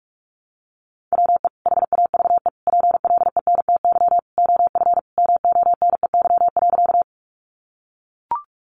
Morse code, heard as a keyed tone, about 44 wpm, 700 Hz. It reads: WE HAVE PLENTY OF MONEY.
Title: Sentences from Top 1300 Words 45wpm